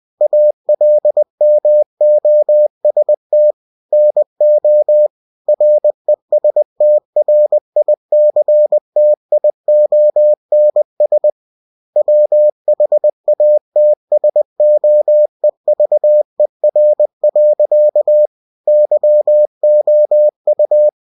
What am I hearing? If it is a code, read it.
Morse code, heard as a keyed tone, about 20 wpm, 600 Hz. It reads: ALMOST NO RESTRICTIONS WHATSOEVER. YOU
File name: winnie-20wpm